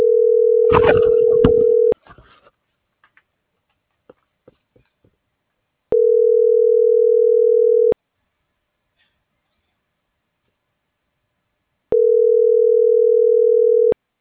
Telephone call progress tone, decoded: ringback tone

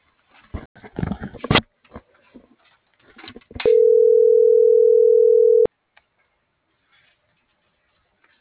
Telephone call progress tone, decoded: ringback tone

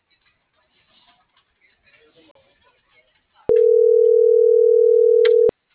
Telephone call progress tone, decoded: ringback tone

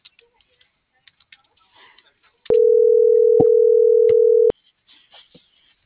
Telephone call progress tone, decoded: ringback tone

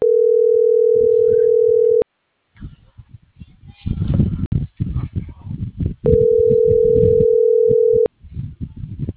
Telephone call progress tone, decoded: ringback tone

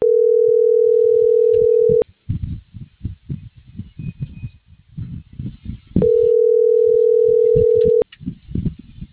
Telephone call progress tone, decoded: ringback tone

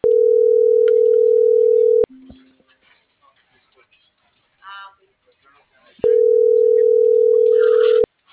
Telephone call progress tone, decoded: ringback tone

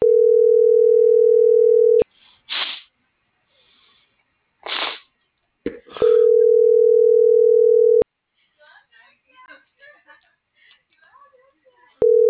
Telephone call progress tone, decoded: ringback tone

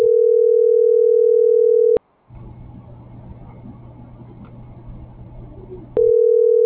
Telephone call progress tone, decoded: ringback tone

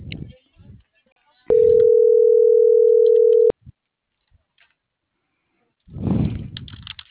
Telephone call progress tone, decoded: ringback tone

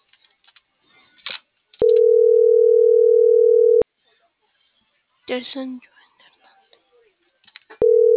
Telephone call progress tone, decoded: ringback tone

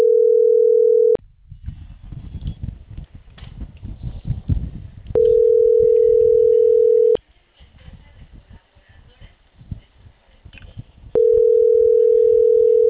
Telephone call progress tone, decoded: ringback tone

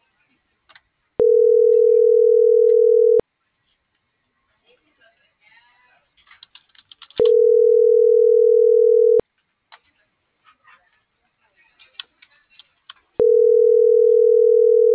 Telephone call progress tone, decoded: ringback tone

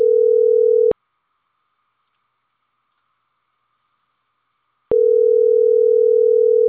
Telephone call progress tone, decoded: ringback tone

